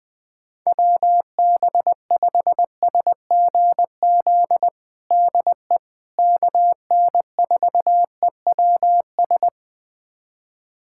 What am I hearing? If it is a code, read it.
Morse code, heard as a keyed tone, about 20 wpm, 700 Hz. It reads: WB5SGZ DE KN4EWS
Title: USA Callsign DE Callsign 20wpm